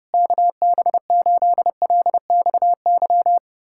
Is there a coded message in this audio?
Yes: Morse code from 0.1 to 3.4 s, heard as a keyed tone, about 34 wpm, 700 Hz. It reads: KB8LXY